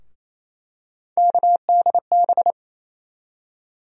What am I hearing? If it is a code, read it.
Morse code, heard as a keyed tone, about 28 wpm, 700 Hz. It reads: KDB
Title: USA Call Sign Postfixes - 3 Letters 28wpm